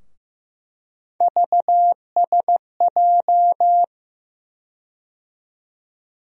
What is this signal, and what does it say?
Morse code, heard as a keyed tone, about 15 wpm, 700 Hz. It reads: VSJ
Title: USA Call Sign Postfixes - 3 Letters 15wpm